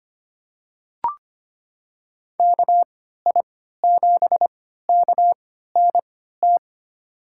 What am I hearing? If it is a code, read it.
Morse code, heard as a keyed tone, about 25 wpm, spaced as for 14 wpm, 700 Hz. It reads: KI7KNT
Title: USA Call Signs - 6 Characters - 3x Character Spacing - 25wpm